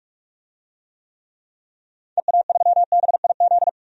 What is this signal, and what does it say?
Morse code, heard as a keyed tone, about 46 wpm, 700 Hz. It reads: EA3BIZ
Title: DXCC-100 Call Signs - 6 Characters - 45wpm